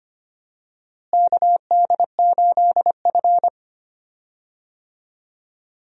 Morse code, heard as a keyed tone, about 25 wpm, 700 Hz. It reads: KD8F
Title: USA Call Signs - 4 Characters - Repeated 2x - 25wpm